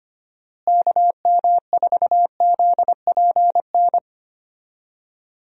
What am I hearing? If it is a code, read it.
Morse code, heard as a keyed tone, about 25 wpm, 700 Hz. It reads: KM4ZPN
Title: USA Call Signs - 6 Characters - Repeated 2x - 25wpm